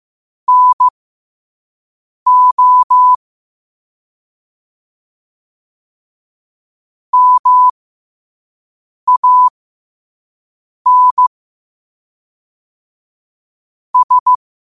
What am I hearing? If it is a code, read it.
Morse code, heard as a keyed tone, about 15 wpm, spaced as for 5 wpm, 1000 Hz. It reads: NO MAN S
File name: 5wpm